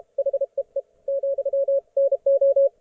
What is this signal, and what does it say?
Morse code, heard as a keyed tone, about 33 wpm, 560 Hz. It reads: HEE , NO